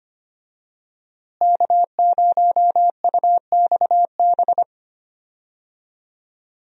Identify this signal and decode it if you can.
Morse code, heard as a keyed tone, about 25 wpm, 690 Hz. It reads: K0UXB
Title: USA Call Signs - 5 Characters - Repeated 2x - 25wpm